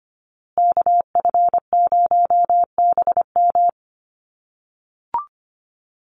Morse code, heard as a keyed tone, about 25 wpm, 700 Hz. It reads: KF0BM
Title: USA Call Signs Repeated 2x 25wpm